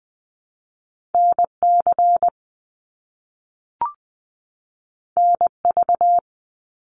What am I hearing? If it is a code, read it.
Morse code, heard as a keyed tone, about 20 wpm, 700 Hz. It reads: NC NV
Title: US States 20wpm